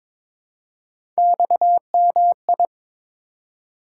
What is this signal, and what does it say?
Morse code, heard as a keyed tone, about 22 wpm, 700 Hz. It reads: XMI
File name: practice / 22wpm / Three Letter-Number Combinations - 22wpm